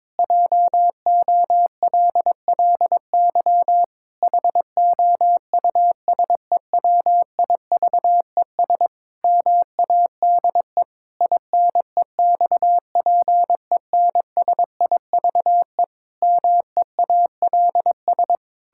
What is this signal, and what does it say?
Morse code, heard as a keyed tone, about 22 wpm, 700 Hz. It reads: JOLLY HOUSEWIVES MADE INEXPENSIVE MEALS
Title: Pangram Challenge v2 - 22wpm